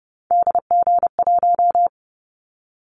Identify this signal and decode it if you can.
Morse code, heard as a keyed tone, about 30 wpm, 700 Hz. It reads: DG1
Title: Callsign Prefixes for Top 25 DXCC - 30wpm